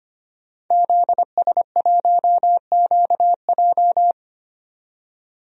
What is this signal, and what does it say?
Morse code, heard as a keyed tone, about 25 wpm, 690 Hz. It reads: ZS1QJ
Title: DXCC-100 Call Signs - 5 Characters - 25wpm